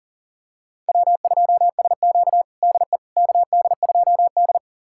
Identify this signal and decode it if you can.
Morse code, heard as a keyed tone, about 40 wpm, 700 Hz. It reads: W2SQ DE KD2D
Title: USA Callsign DE Callsign - 4 Characters - 40wpm